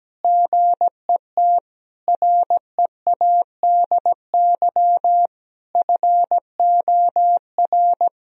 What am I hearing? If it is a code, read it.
Morse code, heard as a keyed tone, about 17 wpm, 700 Hz. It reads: GET READY FOR
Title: Sentences from Top 1000 Words - Review Part 1 - 17wpm